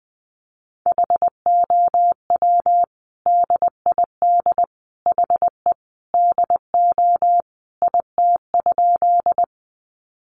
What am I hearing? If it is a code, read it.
Morse code, heard as a keyed tone, about 20 wpm, 700 Hz. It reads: HOW DID HE DO IT?
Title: Sentences from Top 100 Words 20wpm